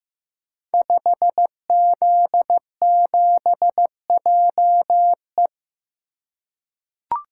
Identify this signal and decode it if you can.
Morse code, heard as a keyed tone, about 15 wpm, 700 Hz. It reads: 5Z7JE
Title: Code Groups - 5 Letter:Numbers - 15wpm